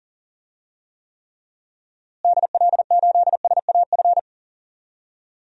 Morse code, heard as a keyed tone, about 40 wpm, 700 Hz. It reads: DL8SAF
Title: DXCC-100 Call Signs - 6 Characters - Repeated 3x - 40wpm